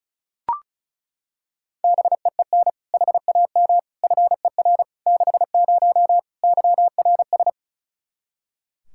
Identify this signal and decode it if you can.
Morse code, heard as a keyed tone, about 34 wpm, 700 Hz. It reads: BEEN HAM FER 60 YRS
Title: Practice QSO - Exchange 3 - More Info - 35wpm